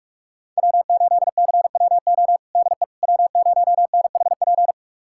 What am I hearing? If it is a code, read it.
Morse code, heard as a keyed tone, about 43 wpm, 700 Hz. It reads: W8CWO DE W0NHP
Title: USA Callsign DE Callsign - 5 Characters - 45wpm